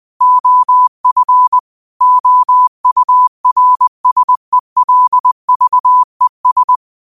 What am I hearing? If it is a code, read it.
Morse code, heard as a keyed tone, about 20 wpm, 1000 Hz. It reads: OF OURSELVES
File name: qotdfiles / 20wpm